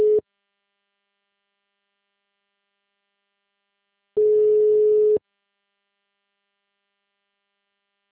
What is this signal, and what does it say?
Ringback tone (Europe, 425 Hz).